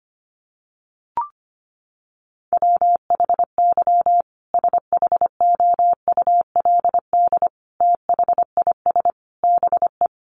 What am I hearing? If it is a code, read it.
Morse code, heard as a keyed tone, about 25 wpm, 700 Hz. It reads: WHY SHOULD THIS BE
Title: Sentences from Top 300 Words 25wpm